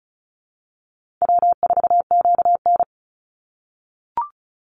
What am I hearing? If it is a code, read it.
Morse code, heard as a keyed tone, about 34 wpm, 700 Hz. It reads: W4QN
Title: Call Signs - 35wpm